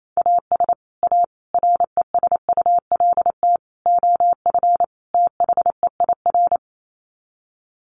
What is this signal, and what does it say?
Morse code, heard as a keyed tone, about 28 wpm, 700 Hz. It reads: AS A RESULT OF THEIR
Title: Sets of 5 Words Encore2 28wpm